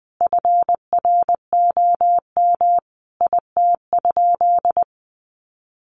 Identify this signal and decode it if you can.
Morse code, heard as a keyed tone, about 20 wpm, 700 Hz. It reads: FROM IT?
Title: Sentences from Top 800 Words 20wpm